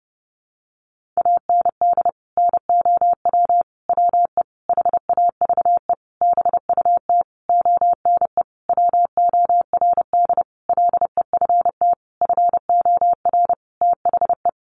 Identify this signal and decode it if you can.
Morse code, heard as a keyed tone, about 30 wpm, 700 Hz. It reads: AND NOW WE HAVE BUT ONE WORD LEFT FOR THE
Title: Sentences from Top 1500 Words 30wpm